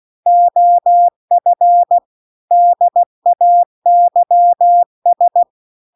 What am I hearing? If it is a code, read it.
Morse code, heard as a keyed tone, about 16 wpm, 700 Hz. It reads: OF DAYS
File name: CW-The-Big-Twoheaded-River-2-16WPM